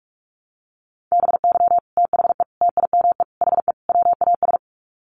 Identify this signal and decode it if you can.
Morse code, heard as a keyed tone, about 44 wpm, 700 Hz. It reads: BY THE TIME HE WAS
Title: Sets of 5 Words Encore 45wpm